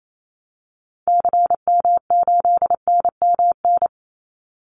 Morse code, heard as a keyed tone, about 28 wpm, 700 Hz. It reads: CM8NMN
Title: Call Signs Repeated 2x - 28wpm